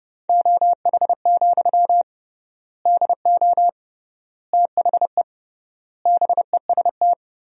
Morse code, heard as a keyed tone, about 30 wpm, 700 Hz. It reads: OH, DO THE BEST
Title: 3x Word-Spacing - Sentences from Top 500 Words - 30wpm